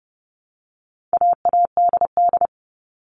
Morse code, heard as a keyed tone, about 30 wpm, 700 Hz. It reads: AADD
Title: All Cut Numbers - 4-Digit 30wpm